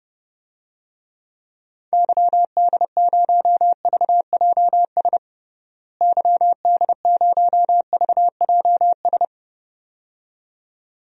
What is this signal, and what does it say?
Morse code, heard as a keyed tone, about 32 wpm, 700 Hz. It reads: YD0VJS YD0VJS
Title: DXCC-100 Call Signs - 6 Characters - Repeated 2x - 30wpm